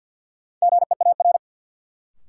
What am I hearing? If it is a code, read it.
Morse code, heard as a keyed tone, about 53 wpm, 700 Hz. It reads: GEAR
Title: Sentences from Top 1400 Words - New Word Drill - 50wpm